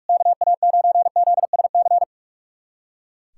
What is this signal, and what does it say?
Morse code, heard as a keyed tone, about 41 wpm, 700 Hz. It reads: KA9ZSC
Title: USA Callsign DE Callsign 45wpm